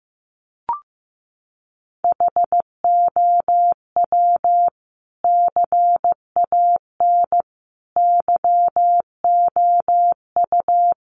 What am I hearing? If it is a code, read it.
Morse code, heard as a keyed tone, about 15 wpm, 700 Hz. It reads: HOW CAN YOU
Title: Sentences from Top 100 Words 15wpm